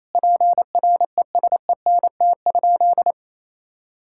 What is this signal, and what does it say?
Morse code, heard as a keyed tone, about 28 wpm, 700 Hz. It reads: PRESENT?
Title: Sentences from Top 1000 Words - Review Part 5 - 28wpm